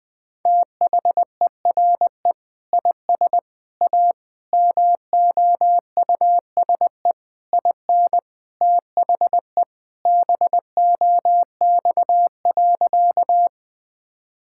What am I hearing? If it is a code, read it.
Morse code, heard as a keyed tone, about 20 wpm, 700 Hz. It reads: THERE IS A MOUSE IN THE BOX.
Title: Sentences from Top 1000 Words - Review Part 5 - 20wpm